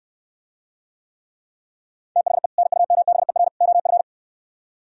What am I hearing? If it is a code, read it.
Morse code, heard as a keyed tone, about 68 wpm, 700 Hz. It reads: THE NUMBER OF